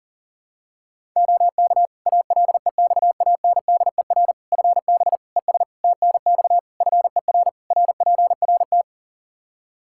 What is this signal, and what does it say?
Morse code, heard as a keyed tone, about 40 wpm, 700 Hz. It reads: OK ALEXANDER FB ES TNX FER RPRT
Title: K1USN SST QSO Practice 40wpm